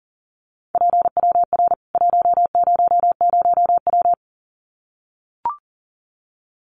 Morse code, heard as a keyed tone, about 40 wpm, 700 Hz. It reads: PWR 100W
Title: Common QSO Phrases 40wpm